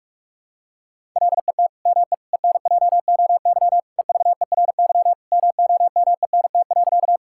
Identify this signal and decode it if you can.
Morse code, heard as a keyed tone, about 46 wpm, 700 Hz. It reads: LET ME ENJOY EVERY MOMENT.
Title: Sentences from Top 1000 Words - Review Part 5 - 45wpm